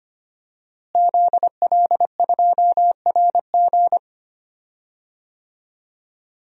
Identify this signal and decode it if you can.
Morse code, heard as a keyed tone, about 25 wpm, 700 Hz. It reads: ZL2RG
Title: DXCC-100 Call Signs - 5 Characters - Repeated 3x - 25wpm